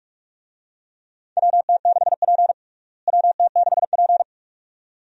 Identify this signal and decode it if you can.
Morse code, heard as a keyed tone, about 46 wpm, 700 Hz. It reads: WT6P WT6P
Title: USA Call Signs - 4 Characters - Repeated 2x - 45wpm